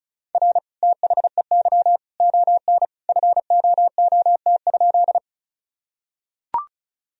Morse code, heard as a keyed tone, about 34 wpm, 700 Hz. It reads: R THEY ON FOOT?
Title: Sentences from Top 500 Words and Abbreviations 35wpm